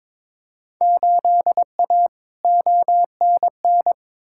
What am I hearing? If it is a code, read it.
Morse code, heard as a keyed tone, about 22 wpm, 700 Hz. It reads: 8A ONN
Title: Field Day Exchanges Encore - 22wpm